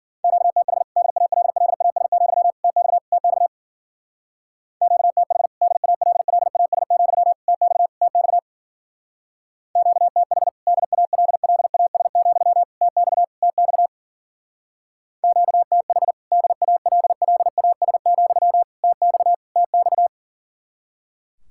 Morse code, heard as a keyed tone, about 50 wpm, 700 Hz. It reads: QTH DALLAS, TX TX QTH DALLAS, TX TX QTH DALLAS, TX TX QTH DALLAS, TX TX